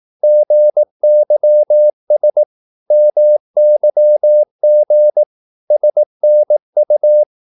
Morse code, heard as a keyed tone, about 18 wpm, 600 Hz. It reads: GYS MYG SNU
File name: Korte ord DK 18wpm